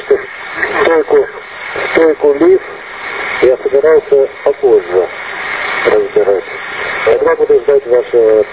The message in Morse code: none

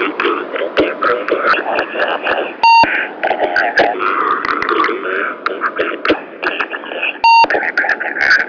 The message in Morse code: none

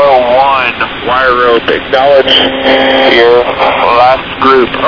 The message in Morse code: none

